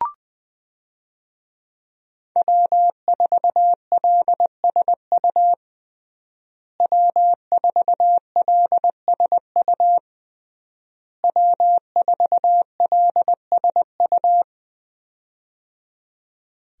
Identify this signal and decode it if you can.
Morse code, heard as a keyed tone, about 20 wpm, 700 Hz. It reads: W4LSU W4LSU W4LSU
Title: USA Call Signs - 5 Characters - Repeated 3x - 20wpm